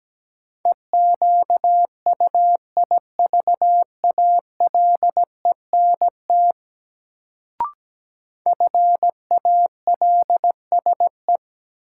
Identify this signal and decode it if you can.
Morse code, heard as a keyed tone, about 17 wpm, 700 Hz. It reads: EQUIVALENT FALSE
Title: Sentences from Top 1600 Words - New Word Drill Alt - 17wpm